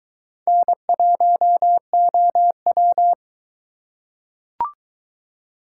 Morse code, heard as a keyed tone, about 22 wpm, 700 Hz. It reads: N1OW